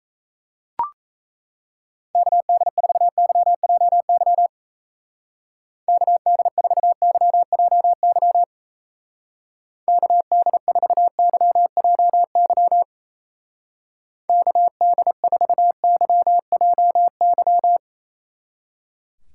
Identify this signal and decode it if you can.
Morse code, heard as a keyed tone, about 36 wpm, 700 Hz. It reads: KD4YJY KD4YJY KD4YJY KD4YJY